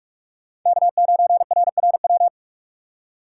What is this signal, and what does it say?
Morse code, heard as a keyed tone, about 47 wpm, 700 Hz. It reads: K9RRW
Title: USA Call Signs - 5 Characters - 6x Character Spacing - 45wpm